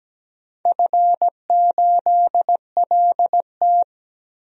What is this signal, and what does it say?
Morse code, heard as a keyed tone, about 17 wpm, 700 Hz. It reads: F8LT
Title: DXCC-100 Call Signs - 4 Characters - 17wpm